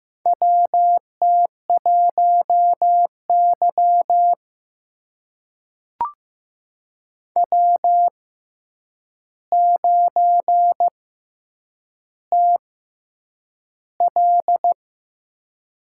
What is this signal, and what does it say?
Morse code, heard as a keyed tone, about 15 wpm, spaced as for 8 wpm, 700 Hz. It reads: WT1Y W9TL
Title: USA Call Signs - 4 Characters - 6x Character Spacing - 15wpm